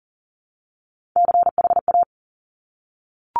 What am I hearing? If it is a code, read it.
Morse code, heard as a keyed tone, about 40 wpm, 700 Hz. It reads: CHA
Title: Top Three Letter Combinations in Words 40wpm